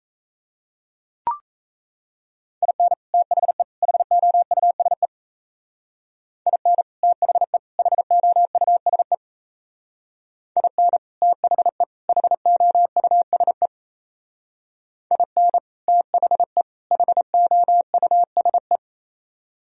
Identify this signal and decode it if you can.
Morse code, heard as a keyed tone, about 33 wpm, 700 Hz. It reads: IN THE HOUSE IN THE HOUSE IN THE HOUSE IN THE HOUSE
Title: Speed-Racing - Sets of 3 Words 28wpm